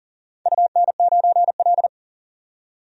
Morse code, heard as a keyed tone, about 42 wpm, 700 Hz. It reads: UN9L
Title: DXCC-100 Call Signs - 4 Characters - 40wpm